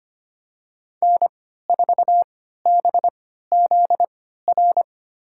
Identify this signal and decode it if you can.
Morse code, heard as a keyed tone, about 25 wpm, spaced as for 14 wpm, 700 Hz. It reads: N4BZR
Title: USA Call Signs - 3x Character Spacing - 25wpm